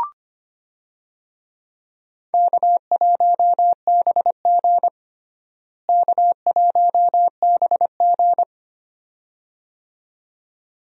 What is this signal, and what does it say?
Morse code, heard as a keyed tone, about 25 wpm, 700 Hz. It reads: K1BG K1BG
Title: USA Call Signs - 4 Characters - Repeated 2x - 25wpm